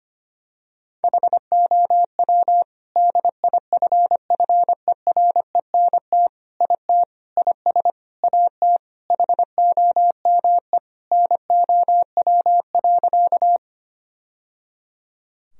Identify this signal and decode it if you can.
Morse code, heard as a keyed tone, about 25 wpm, 700 Hz. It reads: HOW DIFFERENT IT IS AT HOME NOW.
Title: Sentences from Top 1000 Words - Review Part 3 - 25wpm